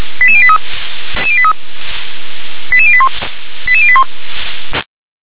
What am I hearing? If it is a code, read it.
Five-tone selective call (ZVEI), heard as quick runs of five tones, tones of about 70 ms. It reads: 88083 88081 88081
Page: SDR/ZVEI